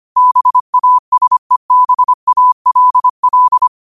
Morse code, heard as a keyed tone, about 25 wpm, 1000 Hz. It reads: BASEBALL
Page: Quotes for Thu, 14 Aug 2025 in Morse Code at 25 words per minute.